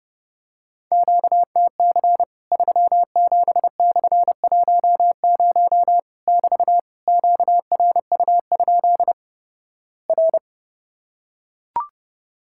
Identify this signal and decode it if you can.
Morse code, heard as a keyed tone, about 30 wpm, 700 Hz. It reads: QTC 37/10 <BT> QRU? R
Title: Worked All Europe DX Contest - QTC Exchange - 30wpm